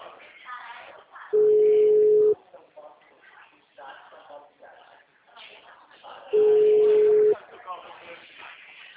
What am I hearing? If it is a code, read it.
Ringback tone (Europe, 425 Hz), with speech.
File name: out- -2001